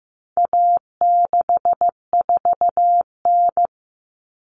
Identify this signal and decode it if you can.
Morse code, heard as a keyed tone, about 15 wpm, 700 Hz. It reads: A64N
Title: Common Cut Numbers - 4-Digit 15wpm